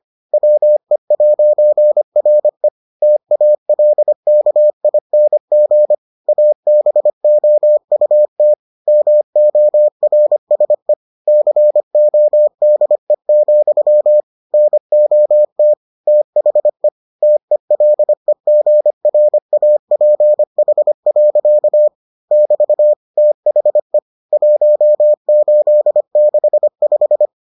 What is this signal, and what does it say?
Morse code, heard as a keyed tone, about 25 wpm, 600 Hz. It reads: WE'RE TALKING ABOUT MORSE CODE, NOT THE TELEGRAPH. <BT> THE 1865